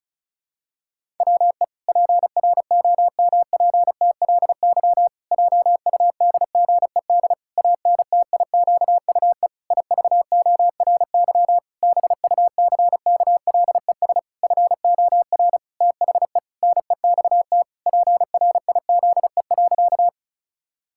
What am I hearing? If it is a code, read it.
Morse code, heard as a keyed tone, about 36 wpm, 700 Hz. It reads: WE PROMPTLY JUDGED ANTIQUE IVORY BUCKLES FOR THE NEXT PRIZE.
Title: Pangram Challenge v2 - 35wpm